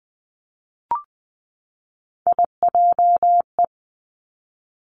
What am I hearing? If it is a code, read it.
Morse code, heard as a keyed tone, about 20 wpm, 700 Hz. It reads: IJE
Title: Three Challenging Letter-Number Combinations 20wpm